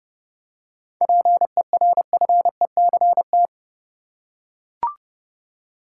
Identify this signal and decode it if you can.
Morse code, heard as a keyed tone, about 30 wpm, 700 Hz. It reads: PERFECT
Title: Seven Letter Words - 30wpm